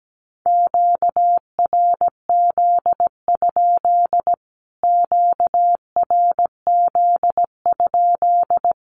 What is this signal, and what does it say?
Morse code, heard as a keyed tone, about 17 wpm, 700 Hz. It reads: QRZ? QRZ?
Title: Q-Codes Repeated 3x 17wpm